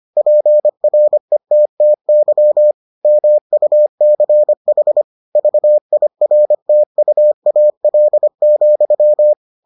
Morse code, heard as a keyed tone, about 25 wpm, 600 Hz. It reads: PRETTY MUCH VIRTUAL,